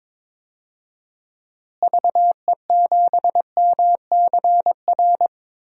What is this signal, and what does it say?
Morse code, heard as a keyed tone, about 22 wpm, 700 Hz. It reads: VE7MCR
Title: DXCC-100 Call Signs - 6 Characters - Repeated 3x - 22wpm